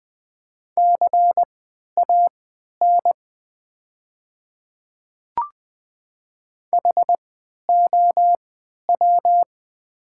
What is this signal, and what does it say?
Morse code, heard as a keyed tone, about 20 wpm, spaced as for 11 wpm, 700 Hz. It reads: CAN HOW
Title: 3x Character Spacing - Top 100 Words - 20wpm